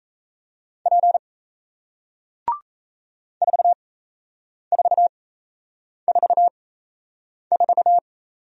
Morse code, heard as a keyed tone, about 38 wpm, spaced as for 8 wpm, 700 Hz. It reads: P 4444